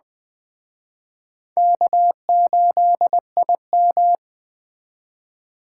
Morse code, heard as a keyed tone, about 21 wpm, 700 Hz. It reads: K8IM
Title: Speed Racing - USA 1x2 Call Signs 17wpm